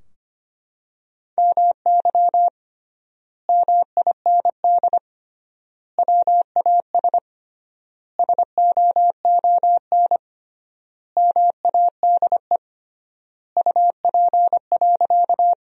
Morse code, heard as a keyed tone, about 25 wpm, 700 Hz. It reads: MY MIND WAS SOON MADE UP.
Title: 3x Word-Spacing - Sentences from Top 500 Words - 25wpm